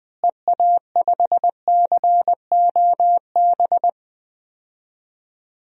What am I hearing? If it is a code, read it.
Morse code, heard as a keyed tone, about 20 wpm, 700 Hz. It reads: EA5COB
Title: DXCC-100 Call Signs - 6 Characters - Repeated 3x - 20wpm